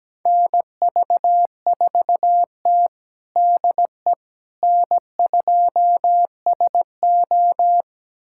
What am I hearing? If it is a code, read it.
Morse code, heard as a keyed tone, about 17 wpm, 700 Hz. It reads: NV4T DE N2SO
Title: USA Callsign DE Callsign - 4 Characters - 17wpm